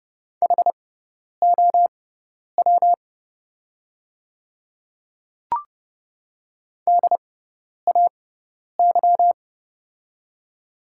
Morse code, heard as a keyed tone, about 30 wpm, spaced as for 10 wpm, 700 Hz. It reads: HOW DAY
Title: 6x Character Spacing - Top 100 Words - 30wpm